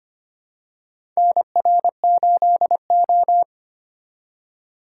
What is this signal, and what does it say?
Morse code, heard as a keyed tone, about 25 wpm, 700 Hz. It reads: NR8O
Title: USA Call Signs - 4 Characters - 8x Character Spacing - 25wpm